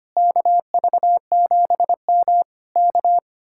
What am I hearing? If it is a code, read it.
Morse code, heard as a keyed tone, about 24 wpm, 700 Hz. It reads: KV7M K
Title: Speed-Racing Common QSO Phrases - 25wpm